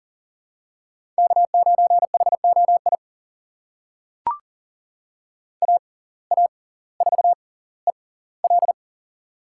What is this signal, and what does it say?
Morse code, heard as a keyed tone, about 38 wpm, spaced as for 20 wpm, 700 Hz. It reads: K9HOI AA4EL
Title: USA Call Signs - 5 Characters - 6x Character Spacing - 40wpm